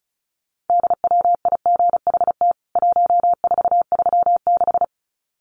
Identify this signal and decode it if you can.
Morse code, heard as a keyed tone, about 34 wpm, 700 Hz. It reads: DWIGHT 1436
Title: CWT Exchange 35wpm